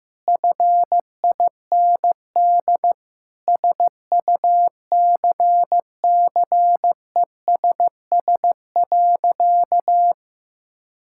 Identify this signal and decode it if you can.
Morse code, heard as a keyed tone, about 15 wpm, 700 Hz. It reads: FIND SUCCESS.
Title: Sentences from Top 1000 Words - Review Part 6 - 15wpm